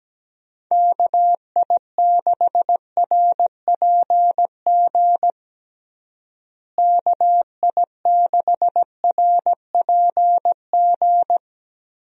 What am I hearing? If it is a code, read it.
Morse code, heard as a keyed tone, about 17 wpm, 700 Hz. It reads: KI6RPG KI6RPG
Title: USA Call Signs - 6 Characters - Repeated 2x - 17wpm